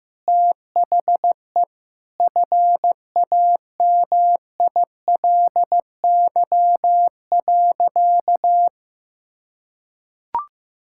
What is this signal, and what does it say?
Morse code, heard as a keyed tone, about 16 wpm, 700 Hz. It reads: THE FAMILY.
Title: Sentences from Top 1000 Words - Review Part 6 - 15wpm